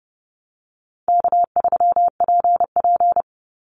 Morse code, heard as a keyed tone, about 30 wpm, 700 Hz. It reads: K3PP
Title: Easy Call Signs Repeated 2x - 30wpm